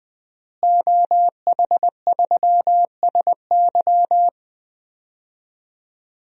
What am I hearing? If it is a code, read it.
Morse code, heard as a keyed tone, about 20 wpm, 700 Hz. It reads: OH3SY
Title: DXCC-100 Call Signs - 5 Characters - Repeated 2x - 20wpm